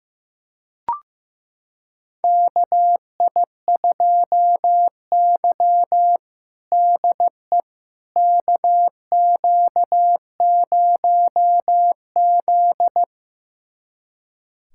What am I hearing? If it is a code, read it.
Morse code, heard as a keyed tone, about 15 wpm, 700 Hz. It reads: KI2Y DE KQ0Z
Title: USA Callsign DE Callsign - 4 Characters - 15wpm